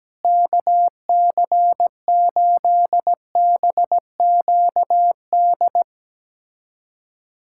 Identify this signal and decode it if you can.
Morse code, heard as a keyed tone, about 17 wpm, 700 Hz. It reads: KC8BQD
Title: USA Callsign DE Callsign - 6 Characters - 17wpm